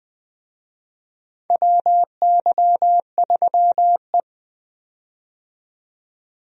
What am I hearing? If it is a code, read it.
Morse code, heard as a keyed tone, about 20 wpm, 700 Hz. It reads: WY3E
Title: USA Call Signs - 4 Characters - 20wpm